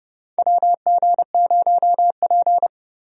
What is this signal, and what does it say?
Morse code, heard as a keyed tone, about 30 wpm, 700 Hz. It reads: WG0P
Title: USA Call Signs - 4 Characters - Repeated 3x - 30wpm